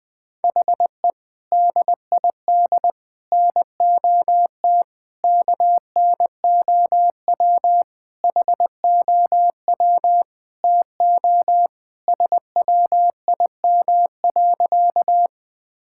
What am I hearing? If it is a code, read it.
Morse code, heard as a keyed tone, about 20 wpm, 700 Hz. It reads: HE DID NOT KNOW HOW TO SWIM.
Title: Sentences with 5-Word Phrases - 20wpm